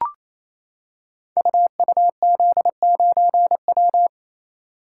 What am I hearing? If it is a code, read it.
Morse code, heard as a keyed tone, about 27 wpm, 700 Hz. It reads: UUZ9W
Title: Code Groups - 5 Letter:Numbers - 28wpm